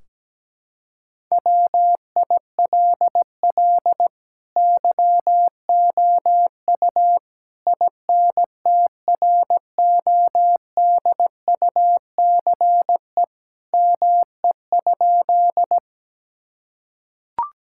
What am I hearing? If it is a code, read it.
Morse code, heard as a keyed tone, about 17 wpm, 700 Hz. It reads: WILL YOU INTRODUCE ME?
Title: Sentences from Top 1600 Words 17wpm